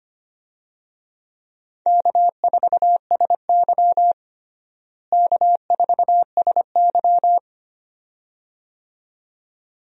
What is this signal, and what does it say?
Morse code, heard as a keyed tone, about 25 wpm, 700 Hz. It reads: K4SY K4SY
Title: USA Call Signs - 4 Characters - Repeated 2x - 25wpm